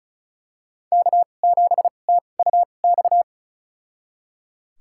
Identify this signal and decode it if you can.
Morse code, heard as a keyed tone, about 36 wpm, spaced as for 25 wpm, 700 Hz. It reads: K7TUX
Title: USA Call Signs - 5 Characters - 2x Character Spacing - 35wpm